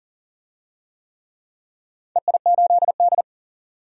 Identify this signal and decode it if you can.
Morse code, heard as a keyed tone, about 40 wpm, 700 Hz. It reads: EI8D
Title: DXCC-100 Call Signs - 4 Characters - 40wpm